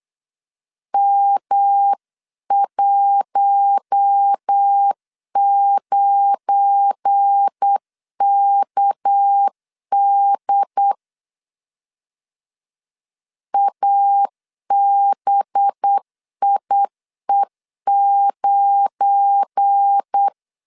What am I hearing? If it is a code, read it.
Morse code, heard as a keyed tone, about 8 wpm, 800 Hz. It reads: M19KD ABIE9
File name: LEZIONI / 7 GF9Z / 7 GF9 8wpm 3545